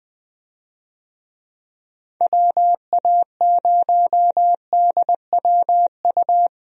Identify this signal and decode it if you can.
Morse code, heard as a keyed tone, about 20 wpm, 700 Hz. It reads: WA0DWU
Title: USA Call Signs - 6 Characters - 20wpm